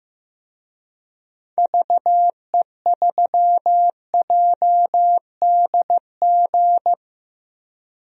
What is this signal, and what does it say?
Morse code, heard as a keyed tone, about 15 wpm, 700 Hz. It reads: VE3JDG
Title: DXCC-100 Call Signs - 6 Characters - Repeated 3x - 15wpm